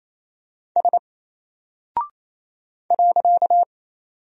Morse code, heard as a keyed tone, about 27 wpm, 700 Hz. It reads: S .
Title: Single Character (Letter:Number:Punctuation:Prosign) 28wpm